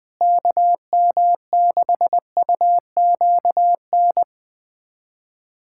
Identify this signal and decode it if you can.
Morse code, heard as a keyed tone, about 21 wpm, 700 Hz. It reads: KM6UQN
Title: USA Callsign DE Callsign - 6 Characters - 20wpm